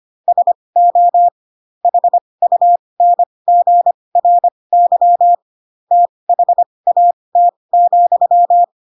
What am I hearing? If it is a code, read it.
Morse code, heard as a keyed tone, about 25 wpm, spaced as for 20 wpm, 700 Hz. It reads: SO HUNGRY THAT,
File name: Pinocchio / Pinocchio ch6 15 25 ik1hge